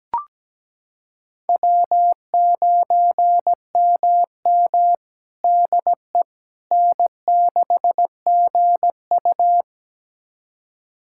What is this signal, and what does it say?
Morse code, heard as a keyed tone, about 17 wpm, 700 Hz. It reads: W9MM DE N6GU
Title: USA Callsign DE Callsign - 4 Characters - 17wpm